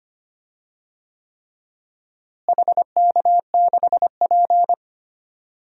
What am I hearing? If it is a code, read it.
Morse code, heard as a keyed tone, about 25 wpm, 700 Hz. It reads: HK6P
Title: DXCC-100 Call Signs - 4 Characters - Repeated 3x - 25wpm